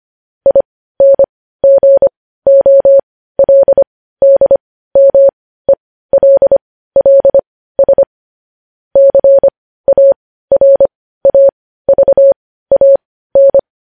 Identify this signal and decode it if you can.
Morse code, heard as a keyed tone, about 25 wpm, spaced as for 15 wpm, 550 Hz. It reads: INGOLDMELLS CARAVAN
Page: news-headlines-in-morse-code-at-15-wpm